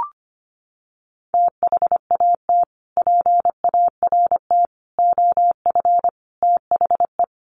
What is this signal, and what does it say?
Morse code, heard as a keyed tone, about 25 wpm, 700 Hz. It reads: THAT PART OF THE
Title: Sets of 5 Words Encore2 25wpm